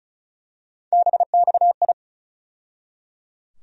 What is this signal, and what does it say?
Morse code, heard as a keyed tone, about 34 wpm, 700 Hz. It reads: BXI
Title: USA Call Sign Postfixes - 3 Letters 35wpm